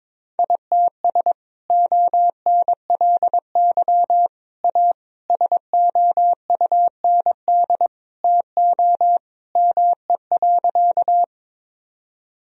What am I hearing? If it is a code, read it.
Morse code, heard as a keyed tone, about 22 wpm, 700 Hz. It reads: ITS ONLY A SOUND TO ME.
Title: Sentences from Top 1000 Words - Review Part 1 - 22wpm